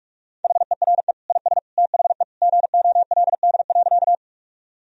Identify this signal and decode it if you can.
Morse code, heard as a keyed tone, about 46 wpm, 700 Hz. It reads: HERE IS THE GOLD.
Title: Sentences from Top 1000 Words - Review Part 3 - 45wpm